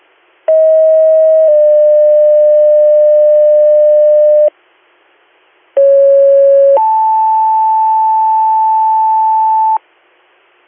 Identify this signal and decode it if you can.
Two-tone page, heard as a one-second tone, then a three-second tone, about 640 then 600 Hz, about 550 then 880 Hz. It is heard from 0.5 to 9.8 s.